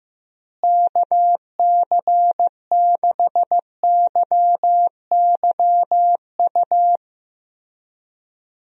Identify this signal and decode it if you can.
Morse code, heard as a keyed tone, about 15 wpm, 700 Hz. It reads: KC6YYU
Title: USA Call Signs - 6 Characters - Repeated 2x - 15wpm